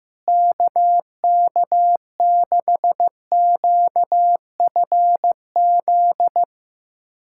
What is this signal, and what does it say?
Morse code, heard as a keyed tone, about 15 wpm, 700 Hz. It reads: KK6QFZ
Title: USA Callsign DE Callsign 15wpm